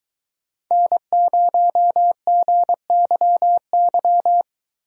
Morse code, heard as a keyed tone, about 22 wpm, 700 Hz. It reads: N0GYY